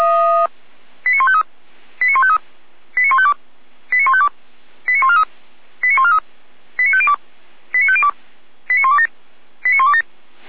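Five-tone selective call (ZVEI), heard as quick runs of five tones, tones of about 70 ms. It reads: 89153 89153 89152 89152 89143 89143 89502 89502 89127 89127
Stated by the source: ZVEI five-tone call 89050 received by SDR, 455 channel.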